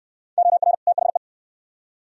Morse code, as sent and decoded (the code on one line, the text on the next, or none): --- .-. / - .... .
OR THE